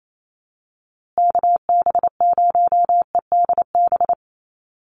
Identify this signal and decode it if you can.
Morse code, heard as a keyed tone, about 28 wpm, 700 Hz. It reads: KB0EDB
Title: USA Call Signs Repeated 3x 28wpm